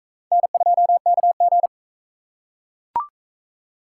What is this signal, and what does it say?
Morse code, heard as a keyed tone, about 41 wpm, 700 Hz. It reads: N2KG